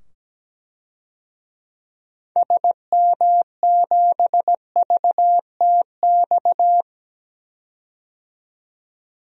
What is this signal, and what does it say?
Morse code, heard as a keyed tone, about 17 wpm, 700 Hz. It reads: SM7VTX
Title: DXCC-100 Call Signs - 6 Characters - 17wpm